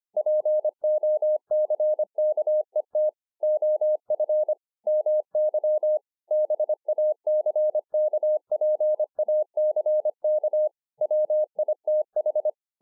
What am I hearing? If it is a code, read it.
Morse code, heard as a keyed tone, about 25 wpm, 610 Hz. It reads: POCKET OF MY BACKPACK WITH